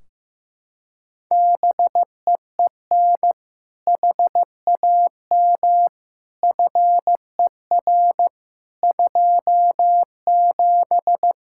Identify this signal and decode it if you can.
Morse code, heard as a keyed tone, about 15 wpm, 700 Hz. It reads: BEEN HAM FER 27
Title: Practice QSO - Exchange 3 - More Info - 15wpm